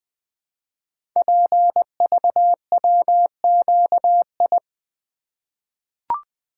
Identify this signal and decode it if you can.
Morse code, heard as a keyed tone, about 20 wpm, 700 Hz. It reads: PVWQI
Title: Code Groups - 5 Letters - 20wpm